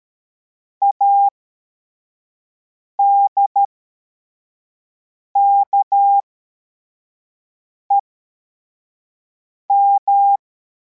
Morse code, heard as a keyed tone, about 13 wpm, spaced as for 4 wpm, 800 Hz. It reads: ADKEM